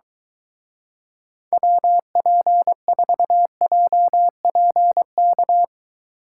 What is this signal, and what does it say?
Morse code, heard as a keyed tone, about 23 wpm, 700 Hz. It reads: WP4JPK